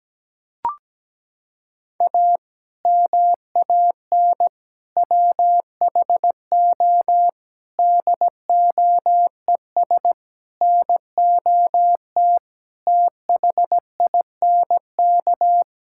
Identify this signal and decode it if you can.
Morse code, heard as a keyed tone, about 17 wpm, 700 Hz. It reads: A MAN WHO DOES NOT THINK
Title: Sentences from Top 1600 Words 17wpm